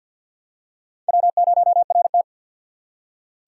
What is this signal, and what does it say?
Morse code, heard as a keyed tone, about 53 wpm, 700 Hz. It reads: W0RT
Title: USA Call Signs - 4 Characters - 6x Character Spacing - 50wpm